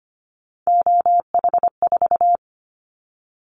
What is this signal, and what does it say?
Morse code, heard as a keyed tone, about 25 wpm, 690 Hz. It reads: OH4
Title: Callsign Prefixes for Top 50 DXCC - 25wpm